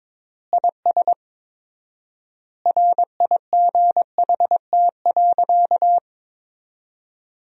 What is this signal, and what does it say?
Morse code, heard as a keyed tone, about 22 wpm, 700 Hz. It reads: IS RIGHT.
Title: 4x Word-Spacing - Sentences from Top 200 Words - 22wpm